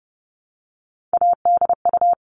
Morse code, heard as a keyed tone, about 30 wpm, 700 Hz. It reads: ADU
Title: All Cut Numbers - 3-Digit 30wpm